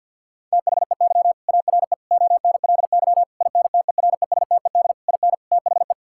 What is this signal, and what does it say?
Morse code, heard as a keyed tone, about 53 wpm, 700 Hz. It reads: THEY ARE ONLY INTERESTED IN THE
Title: Sentences from Top 1000 Words - Review Part 6 - 50wpm